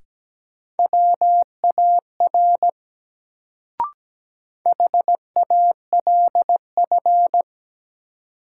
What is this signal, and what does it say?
Morse code, heard as a keyed tone, about 17 wpm, 700 Hz. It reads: WAR HALF
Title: Sentences from Top 400 Words - New Word Drill - 17wpm